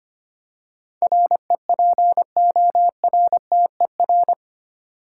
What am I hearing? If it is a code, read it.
Morse code, heard as a keyed tone, about 25 wpm, 700 Hz. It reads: REPORTER
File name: Eight Letter Words - 25wpm